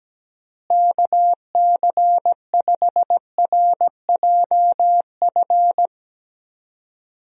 Morse code, heard as a keyed tone, about 17 wpm, 700 Hz. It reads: KC5RJF
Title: USA Call Signs - 6 Characters - Repeated 2x - 17wpm